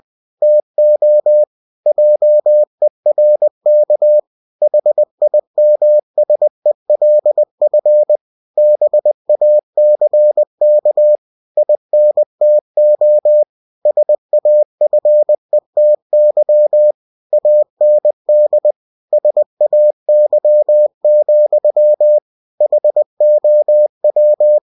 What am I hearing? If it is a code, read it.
Morse code, heard as a keyed tone, about 20 wpm, 600 Hz. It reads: TO JERK HIMSELF BACK INTO SAFETY AND SAY, HOW
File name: cw-books / winnie-the-pooh / winnie-20wpm-page-0101